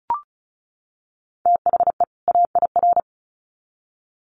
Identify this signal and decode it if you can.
Morse code, heard as a keyed tone, about 34 wpm, 700 Hz. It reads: THE AIR
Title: Sets of 2 Words 35wpm